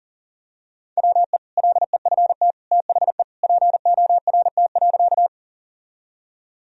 Morse code, heard as a keyed tone, about 40 wpm, 700 Hz. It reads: WE LEFT THE PORT.
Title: Sentences from Top 1000 Words - Review Part 1 - 40wpm